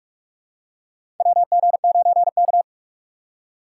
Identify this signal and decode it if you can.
Morse code, heard as a keyed tone, about 45 wpm, 700 Hz. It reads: WG9K
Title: USA Call Signs - 4 Characters - 4x Character Spacing - 45wpm